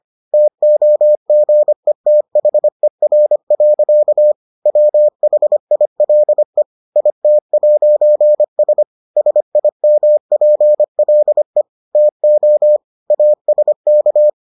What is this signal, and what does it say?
Morse code, heard as a keyed tone, about 25 wpm, 600 Hz. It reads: TOGETHER. WHILE IT'S SIMPLE TO ASK